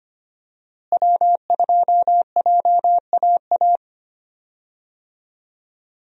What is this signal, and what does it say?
Morse code, heard as a keyed tone, about 25 wpm, 700 Hz. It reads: W2JAA
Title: USA Call Signs - 5 Characters - Repeated 2x - 25wpm